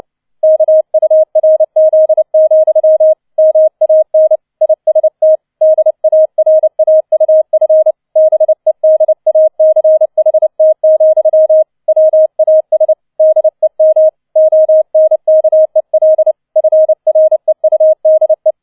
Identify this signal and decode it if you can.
Morse code, heard as a keyed tone, about 29 wpm, 630 Hz. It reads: KURZ, MAN IST DARAUF BEDACHT, WAS DEM ONKEL FREUDE